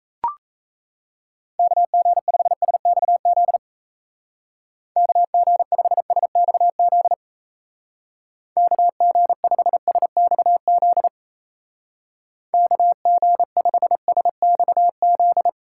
Morse code, heard as a keyed tone, about 34 wpm, 700 Hz. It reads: KG5SXZ KG5SXZ KG5SXZ KG5SXZ
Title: Speed Racing - USA 2x3 Call Signs 28wpm